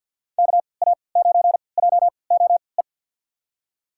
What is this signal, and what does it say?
Morse code, heard as a keyed tone, about 47 wpm, spaced as for 28 wpm, 700 Hz. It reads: KA9JOE
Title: USA Call Signs - 6 Characters - 3x Character Spacing - 50wpm